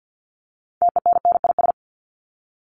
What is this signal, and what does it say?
Morse code, heard as a keyed tone, about 50 wpm, 700 Hz. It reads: TENNIS
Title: Top 1501-1750 Words 50wpm